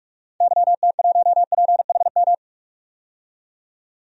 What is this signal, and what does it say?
Morse code, heard as a keyed tone, about 46 wpm, 700 Hz. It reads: YT1PHM
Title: DXCC-100 Call Signs - 6 Characters - 45wpm